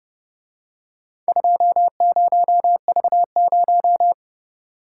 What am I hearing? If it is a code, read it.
Morse code, heard as a keyed tone, about 30 wpm, 700 Hz. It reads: 20V0